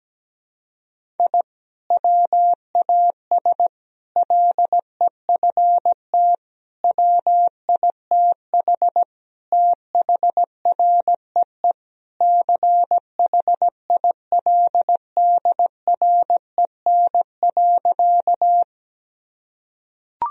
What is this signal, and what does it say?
Morse code, heard as a keyed tone, about 17 wpm, 700 Hz. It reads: I WAS LEFT WITH THREE CHILDREN.
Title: Sentences from Top 1000 Words - Review Part 1 - 17wpm